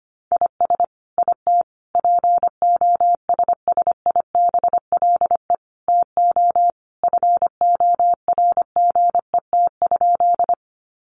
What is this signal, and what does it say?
Morse code, heard as a keyed tone, about 25 wpm, 700 Hz. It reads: IS IT POSSIBLE TO FORGET?
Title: Sentences from Top 1200 Words 25wpm